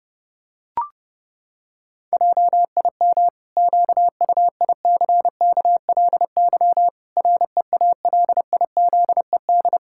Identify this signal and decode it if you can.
Morse code, heard as a keyed tone, about 30 wpm, 700 Hz. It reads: JIM QUICKLY REALIZED
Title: Pangram Challenge v2 - 30wpm